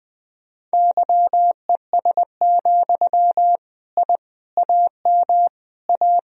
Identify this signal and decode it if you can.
Morse code, heard as a keyed tone, about 21 wpm, 700 Hz. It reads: YES, I AM A
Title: Sentences from Top 1000 Words - Review Part 2 - 20wpm